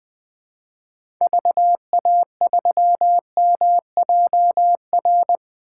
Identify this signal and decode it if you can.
Morse code, heard as a keyed tone, about 20 wpm, 700 Hz. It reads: VA3MJR
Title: 2023 POTA Hunter Call Signs - 20wpm